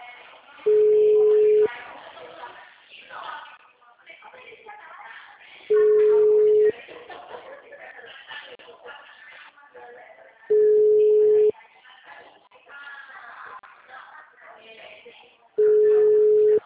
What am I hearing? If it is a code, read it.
Ringback tone (Europe, 425 Hz), with speech.
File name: out- -2041- - - .31236